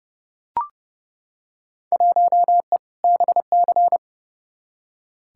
Morse code, heard as a keyed tone, about 30 wpm, 700 Hz. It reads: 1E BC
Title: Field Day Exchanges - 30wpm